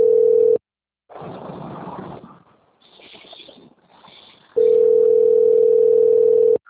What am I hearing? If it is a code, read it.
Ringback tone (North America), 440 + 480 Hz.